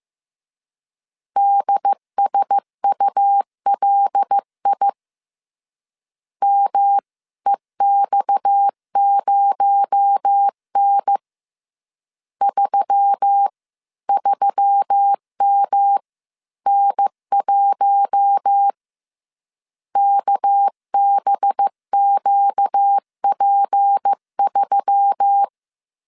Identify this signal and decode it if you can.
Morse code, heard as a keyed tone, about 15 wpm, 800 Hz. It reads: DSULI MEX0N 33MN1 KBQP3